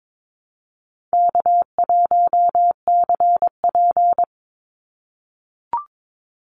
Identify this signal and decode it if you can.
Morse code, heard as a keyed tone, about 22 wpm, 700 Hz. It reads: K1CP
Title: Easy Call Signs - 22wpm